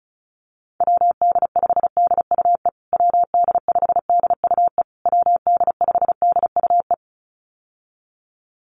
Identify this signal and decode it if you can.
Morse code, heard as a keyed tone, about 34 wpm, 700 Hz. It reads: WD5DUE WD5DUE WD5DUE
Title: USA Call Signs Repeated 3x 35wpm